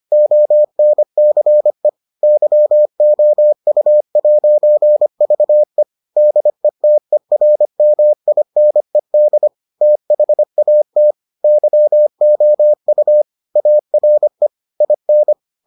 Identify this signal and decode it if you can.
Morse code, heard as a keyed tone, about 25 wpm, 600 Hz. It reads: ONCE YOU'VE DETERMINED THAT YOU ARE IN